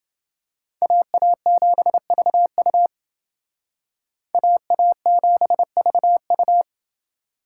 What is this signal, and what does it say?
Morse code, heard as a keyed tone, about 29 wpm, 700 Hz. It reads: AA7VU AA7VU